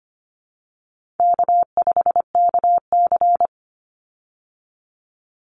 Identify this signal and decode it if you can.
Morse code, heard as a keyed tone, about 25 wpm, 700 Hz. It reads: K5KC
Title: Easy Call Signs Repeated 3x - 25wpm